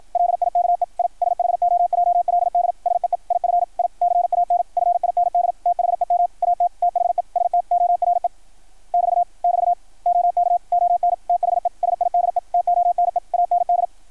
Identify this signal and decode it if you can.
Morse code, heard as a keyed tone, about 53 wpm, 660 Hz. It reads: LIKE I SHOJLD SEE IF I CAN FIND THEM AT THE STORE <BT> <BT> OK ON THE SIDE TONE AND
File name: archived / CRSnet